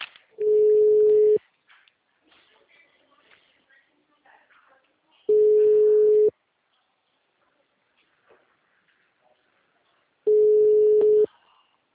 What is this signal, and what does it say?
Ringback tone (Europe, 425 Hz).